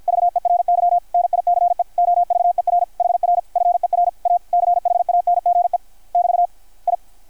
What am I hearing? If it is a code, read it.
Morse code, heard as a keyed tone, about 52 wpm, 700 Hz. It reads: VERY NICE OVER HR FER A CHANGE <BT> I
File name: archived / CRSnet